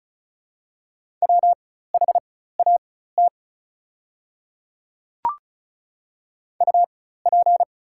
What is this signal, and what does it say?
Morse code, heard as a keyed tone, about 35 wpm, spaced as for 16 wpm, 700 Hz. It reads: WHAT UP
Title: 4x Character Spacing - Top 100 Words - 35wpm